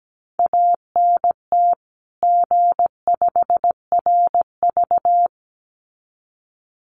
Morse code, heard as a keyed tone, about 17 wpm, 700 Hz. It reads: ANT G5RV
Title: Common QSO Phrases 17wpm